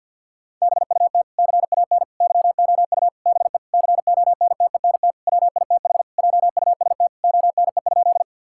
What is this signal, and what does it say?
Morse code, heard as a keyed tone, about 46 wpm, 700 Hz. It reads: BUT CAN YOU BE CONTENT WITH JUST ONE?
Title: Sentences from Top 1000 Words - Review Part 6 - 50wpm